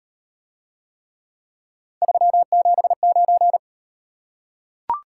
Morse code, heard as a keyed tone, about 38 wpm, 700 Hz. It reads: 379